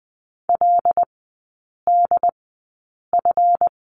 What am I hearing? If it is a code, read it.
Morse code, heard as a keyed tone, about 20 wpm, spaced as for 8 wpm, 700 Hz. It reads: LDF
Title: 138 Lesson 23e - Letter f - Three Characters 20wpm